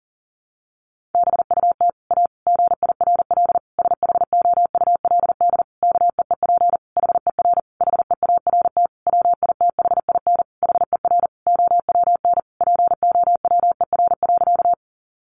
Morse code, heard as a keyed tone, about 40 wpm, 700 Hz. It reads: BUT A GIRL SHOULD KEEP HER HEART WITHIN HER OWN POWER.
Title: Sentences from Top 1200 Words 40wpm